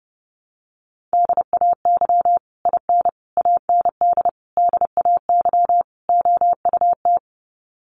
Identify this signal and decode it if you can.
Morse code, heard as a keyed tone, about 30 wpm, 700 Hz. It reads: DAY IN AND DAY OUT
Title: Sets of 5 Words Encore2 30wpm